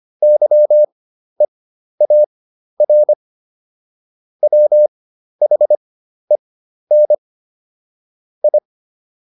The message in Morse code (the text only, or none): YEAR WHEN I